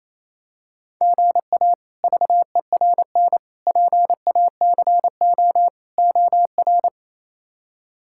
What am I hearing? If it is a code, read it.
Morse code, heard as a keyed tone, about 28 wpm, 700 Hz. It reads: GA VERN PACO OR
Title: K1USN SST QSO Practice - 28wpm